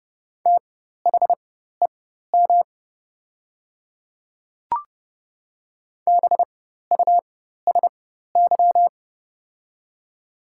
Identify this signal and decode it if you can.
Morse code, heard as a keyed tone, about 30 wpm, spaced as for 14 wpm, 700 Hz. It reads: THEM BUSY
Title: 4x Character Spacing - Top 500 Words - 30wpm